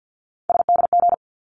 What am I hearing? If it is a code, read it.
Morse code, heard as a keyed tone, about 48 wpm, 700 Hz. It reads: SDG
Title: ARRL Sections 50wpm